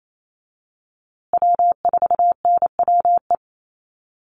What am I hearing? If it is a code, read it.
Morse code, heard as a keyed tone, about 28 wpm, 700 Hz. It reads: W4NWE
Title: USA Call Signs Repeated 2x 28wpm